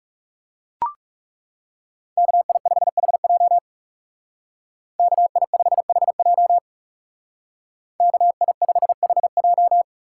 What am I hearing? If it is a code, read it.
Morse code, heard as a keyed tone, about 39 wpm, 700 Hz. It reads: KI5HJ KI5HJ KI5HJ